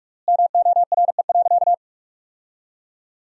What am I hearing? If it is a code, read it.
Morse code, heard as a keyed tone, about 46 wpm, 700 Hz. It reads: MORE.
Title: 3x Word-Spacing - Sentences from Top 300 Words - 45wpm